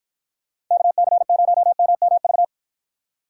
Morse code, heard as a keyed tone, about 49 wpm, 700 Hz. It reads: KC0MMV